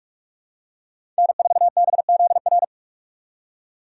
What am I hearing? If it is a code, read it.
Morse code, heard as a keyed tone, about 46 wpm, 700 Hz. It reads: N4BZR
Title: USA Call Signs - 3x Character Spacing - 45wpm